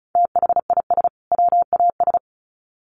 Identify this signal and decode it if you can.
Morse code, heard as a keyed tone, about 34 wpm, 700 Hz. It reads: THIS WAS
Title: Sets of 2 Words 35wpm